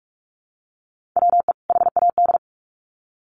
Morse code, heard as a keyed tone, about 44 wpm, 700 Hz. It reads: WE HAD
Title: Sets of 2 Words 45wpm